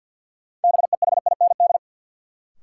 Morse code, heard as a keyed tone, about 51 wpm, 700 Hz. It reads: BEHIND
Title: Sentences from Top 500 Words - New Word Drill - 50wpm